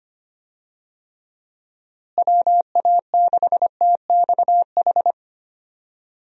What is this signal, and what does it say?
Morse code, heard as a keyed tone, about 25 wpm, 700 Hz. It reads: WA6TXH
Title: USA Call Signs - 6 Characters - Repeated 2x - 25wpm